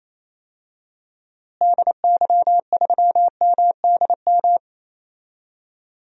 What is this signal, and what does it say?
Morse code, heard as a keyed tone, about 28 wpm, 700 Hz. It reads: DY3MDM
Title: DXCC-100 Call Signs - 6 Characters - Repeated 2x - 28wpm